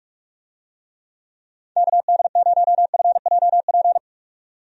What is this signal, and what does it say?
Morse code, heard as a keyed tone, about 41 wpm, 700 Hz. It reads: KD0FJP
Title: USA Call Signs - 6 Characters - 45wpm